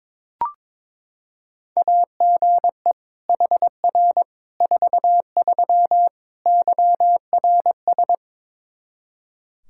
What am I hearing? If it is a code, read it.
Morse code, heard as a keyed tone, about 21 wpm, 700 Hz. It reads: AGE HR 43 YRS
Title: Practice QSO - Exchange 3 - More Info - 22wpm